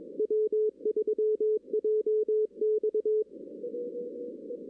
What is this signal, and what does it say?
Morse code, heard as a keyed tone, about 22 wpm, 410 Hz. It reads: W3JX
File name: 2016 arrl dx cw / 2016-ARRL-DX-CW-K5TR- - 0-22